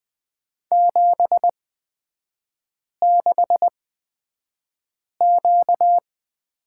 Morse code, heard as a keyed tone, about 20 wpm, spaced as for 5 wpm, 700 Hz. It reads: 76Q
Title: 236 Lesson 38e - Letter q - Three Characters 20wpm